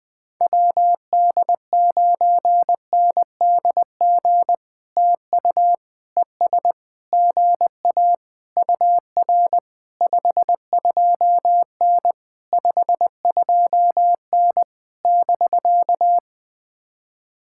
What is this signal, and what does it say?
Morse code, heard as a keyed tone, about 19 wpm, 700 Hz. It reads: WD9NDG TU ES GA UR 52N 52N <BK>
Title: POTA QSO Practice - 20wpm